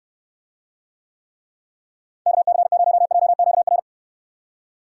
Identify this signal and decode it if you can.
Morse code, heard as a keyed tone, about 69 wpm, 700 Hz. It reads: KC0JQR